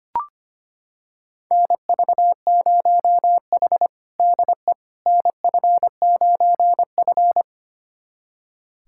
Morse code, heard as a keyed tone, about 25 wpm, 700 Hz. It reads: NV0H DE NF9F
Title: USA Callsign DE Callsign - 4 Characters - 25wpm